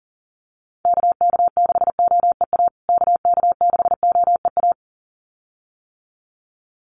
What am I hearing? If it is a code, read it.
Morse code, heard as a keyed tone, about 40 wpm, 700 Hz. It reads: KK6OEA KK6OEA
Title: USA Call Signs Repeated 2x 40wpm